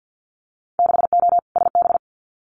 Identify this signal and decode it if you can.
Morse code, heard as a keyed tone, about 50 wpm, 700 Hz. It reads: 6O SB
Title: Winter Field Day Exchanges Encore 50wpm